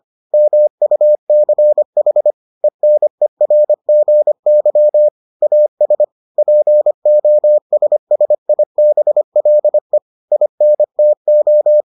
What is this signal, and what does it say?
Morse code, heard as a keyed tone, about 25 wpm, 600 Hz. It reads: MUCH ENERGY AS POSSIBLE INTO